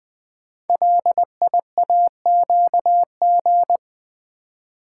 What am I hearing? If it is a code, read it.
Morse code, heard as a keyed tone, about 20 wpm, 700 Hz. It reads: LIAQG
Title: Code Groups - 5 Letters - 20wpm